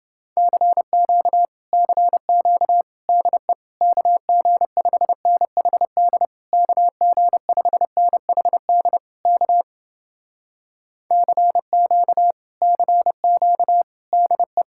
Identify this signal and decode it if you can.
Morse code, heard as a keyed tone, about 30 wpm, 700 Hz. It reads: CQ CQ DE KG5NHD KG5NHD K CQ CQ DE